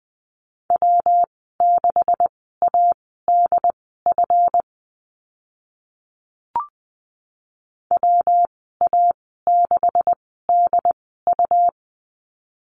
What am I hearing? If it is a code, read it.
Morse code, heard as a keyed tone, about 19 wpm, spaced as for 14 wpm, 700 Hz. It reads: W6ADF WA6DU
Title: 186 Lesson 30g - Letter b - Call Signs 20wpm